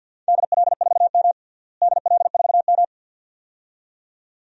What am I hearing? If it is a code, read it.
Morse code, heard as a keyed tone, about 47 wpm, 700 Hz. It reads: DL4M DL4M
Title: DXCC-100 Call Signs - 4 Characters - Repeated 2x - 50wpm